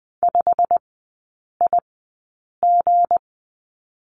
Morse code, heard as a keyed tone, about 20 wpm, spaced as for 8 wpm, 690 Hz. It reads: 5IG
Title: 172 Lesson 28e - Number 9 - Three Characters 20wpm